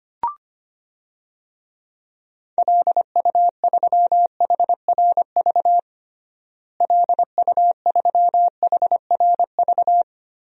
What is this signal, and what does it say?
Morse code, heard as a keyed tone, about 25 wpm, 700 Hz. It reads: LU3HRV LU3HRV
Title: DXCC-100 Call Signs - 6 Characters - Repeated 3x - 25wpm